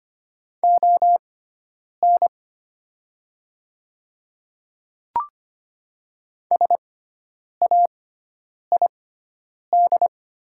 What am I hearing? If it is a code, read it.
Morse code, heard as a keyed tone, about 25 wpm, spaced as for 9 wpm, 700 Hz. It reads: ON SAID
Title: 6x Character Spacing - Top 100 Words - 25wpm